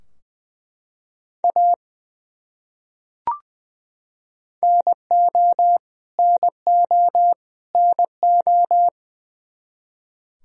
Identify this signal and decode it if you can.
Morse code, heard as a keyed tone, about 20 wpm, 700 Hz. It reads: A NO NO NO
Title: Top 100 Words Repeated 20wpm